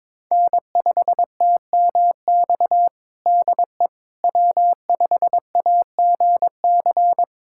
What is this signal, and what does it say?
Morse code, heard as a keyed tone, about 22 wpm, 700 Hz. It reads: N5TMX DE W5AGC
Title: USA Callsign DE Callsign - 5 Characters - 22wpm